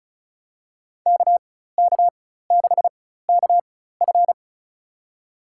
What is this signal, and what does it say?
Morse code, heard as a keyed tone, about 36 wpm, spaced as for 16 wpm, 700 Hz. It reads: KK6KF
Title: USA Call Signs - 5 Characters - 4x Character Spacing - 35wpm